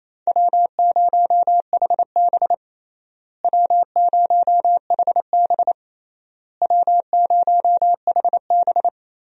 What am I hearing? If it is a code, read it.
Morse code, heard as a keyed tone, about 27 wpm, 700 Hz. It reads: W0HB W0HB W0HB
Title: USA Call Signs - 4 Characters - Repeated 3x - 28wpm